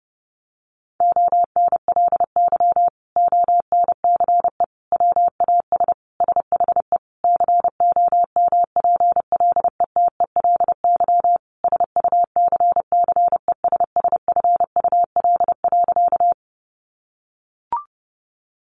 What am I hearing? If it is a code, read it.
Morse code, heard as a keyed tone, about 30 wpm, 700 Hz. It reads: ONLY ONCE WAS SHE COMPLETELY SUCCESSFUL.
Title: Sentences from Top 1100 Words 30wpm